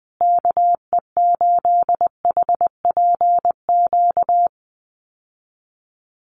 Morse code, heard as a keyed tone, about 20 wpm, 700 Hz. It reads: KE8HPQ
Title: USA Call Signs Repeated 3x 20wpm